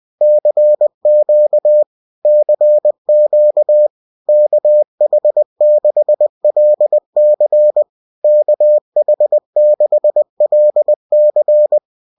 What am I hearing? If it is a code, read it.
Morse code, heard as a keyed tone, about 20 wpm, 600 Hz. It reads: CQ CQ KH6LC KH6LC